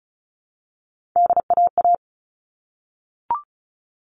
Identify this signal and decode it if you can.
Morse code, heard as a keyed tone, about 34 wpm, 700 Hz. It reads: DAA
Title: All Cut Numbers - 3-Digit 35wpm